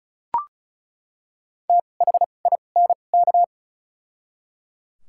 Morse code, heard as a keyed tone, about 34 wpm, spaced as for 25 wpm, 700 Hz. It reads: THINK
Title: 2x Character Spacing - Top 500 Words - 35wpm